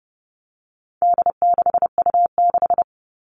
Morse code, heard as a keyed tone, about 30 wpm, 700 Hz. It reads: D6U6
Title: All Cut Numbers - 4-Digit 30wpm